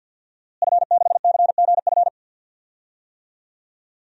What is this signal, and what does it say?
Morse code, heard as a keyed tone, about 51 wpm, 700 Hz. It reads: F6CGF
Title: DXCC-100 Call Signs - 5 Characters - Repeated 2x - 50wpm